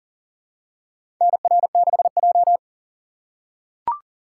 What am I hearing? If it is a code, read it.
Morse code, heard as a keyed tone, about 40 wpm, 700 Hz. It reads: NR6J
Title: USA Call Signs - 4 Characters - 4x Character Spacing - 40wpm